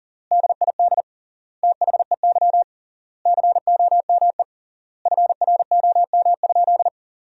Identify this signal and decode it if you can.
Morse code, heard as a keyed tone, about 40 wpm, 700 Hz. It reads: DID THEY COME FROM?
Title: 3x Word-Spacing - Sentences from Top 200 Words - 40wpm